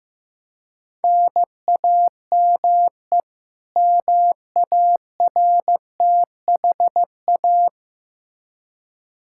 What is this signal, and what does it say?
Morse code, heard as a keyed tone, about 15 wpm, 700 Hz. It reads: NAME MARTHA
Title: Practice QSO - Exchange 1 - Beginning - 15wpm